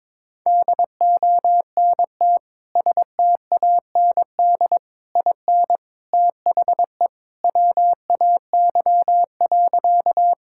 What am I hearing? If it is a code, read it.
Morse code, heard as a keyed tone, about 22 wpm, 700 Hz. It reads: DONT STAND IN THE WAY.
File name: practice / 22wpm / Sentences with 3-Word Phrases - 22wpm